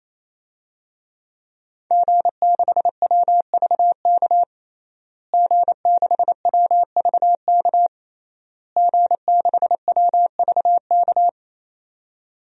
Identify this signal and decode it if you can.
Morse code, heard as a keyed tone, about 28 wpm, 700 Hz. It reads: G6WVK G6WVK G6WVK
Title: DXCC-100 Call Signs - 5 Characters - Repeated 3x - 28wpm